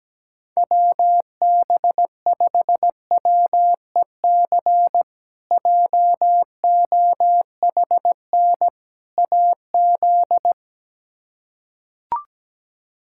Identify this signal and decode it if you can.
Morse code, heard as a keyed tone, about 17 wpm, 700 Hz. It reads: WB5WEC JOHN AZ
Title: K1USN SST QSO Practice - 17wpm